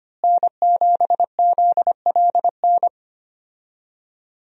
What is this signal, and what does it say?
Morse code, heard as a keyed tone, about 25 wpm, 700 Hz. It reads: N7ZLN
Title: USA Call Signs - 5 Characters - Repeated 3x - 25wpm